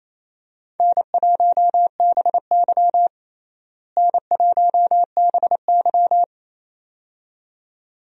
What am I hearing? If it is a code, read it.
Morse code, heard as a keyed tone, about 28 wpm, 700 Hz. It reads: N1BY N1BY
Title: USA Call Signs - 4 Characters - Repeated 2x - 28wpm